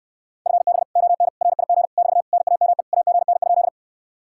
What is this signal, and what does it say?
Morse code, heard as a keyed tone, about 69 wpm, 700 Hz. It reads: FB ON RIG <BT> NAME AGN?